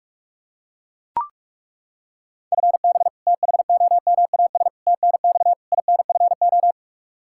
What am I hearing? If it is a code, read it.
Morse code, heard as a keyed tone, about 41 wpm, 700 Hz. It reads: FB THOMAS TNX INFO
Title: Practice QSO - Exchange 3 - More Info - 45wpm